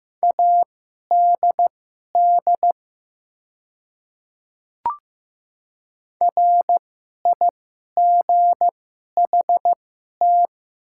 Morse code, heard as a keyed tone, about 15 wpm, spaced as for 11 wpm, 700 Hz. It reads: ADD RIGHT
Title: 2x Character Spacing - Top 200 Words - 15wpm